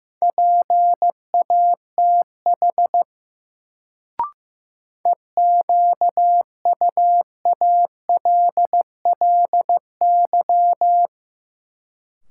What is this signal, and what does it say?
Morse code, heard as a keyed tone, about 15 wpm, 700 Hz. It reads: PATH EQUALLY
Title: Sentences from Top 1400 Words - New Word Drill - 15wpm